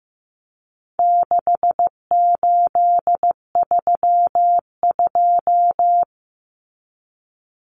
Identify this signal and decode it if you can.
Morse code, heard as a keyed tone, about 15 wpm, 700 Hz. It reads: 6832
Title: Numbers - 4-Digit 15wpm